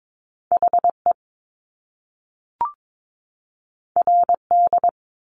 Morse code, heard as a keyed tone, about 22 wpm, 700 Hz. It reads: HE RD
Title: Top Two Letter Combinations in Words 22wpm